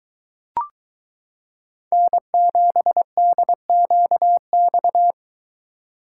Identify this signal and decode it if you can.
Morse code, heard as a keyed tone, about 23 wpm, 700 Hz. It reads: N7DQX